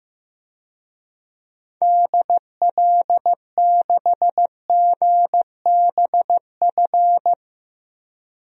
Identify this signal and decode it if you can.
Morse code, heard as a keyed tone, about 15 wpm, 700 Hz. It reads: DL6GBF
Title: DXCC-100 Call Signs - 6 Characters - Repeated 2x - 15wpm